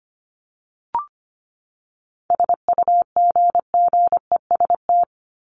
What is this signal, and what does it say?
Morse code, heard as a keyed tone, about 25 wpm, 700 Hz. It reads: SUGGEST
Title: Top 1251-1500 Words 25wpm